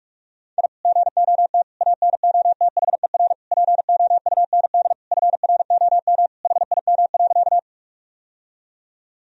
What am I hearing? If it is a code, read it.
Morse code, heard as a keyed tone, about 41 wpm, 700 Hz. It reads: I GOT ANOTHER POUND FROM HIM.
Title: Sentences from Top 1000 Words - Review Part 2 - 45wpm